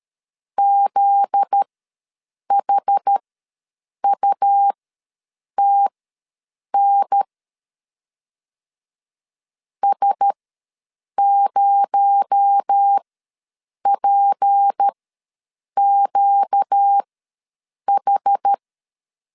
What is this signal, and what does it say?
Morse code, heard as a keyed tone, about 13 wpm, spaced as for 7 wpm, 800 Hz. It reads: ZHUTN S0PQH
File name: LEZIONI / 8 YJ45 / 8 YJ4 ws 12wpm 3249